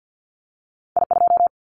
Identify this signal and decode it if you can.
Morse code, heard as a keyed tone, about 50 wpm, 700 Hz. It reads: I2
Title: Callsign Prefixes for Top 25 DXCC - Repeated 3x - 50wpm